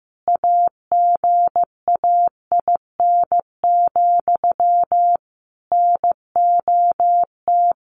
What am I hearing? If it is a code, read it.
Morse code, heard as a keyed tone, about 15 wpm, 700 Hz. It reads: AGAIN, NOT
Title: A Sound Of Thunder 15wpm